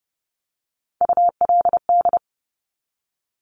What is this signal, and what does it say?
Morse code, heard as a keyed tone, about 30 wpm, 690 Hz. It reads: ULD
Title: Top Three Letter Combinations in Words 30wpm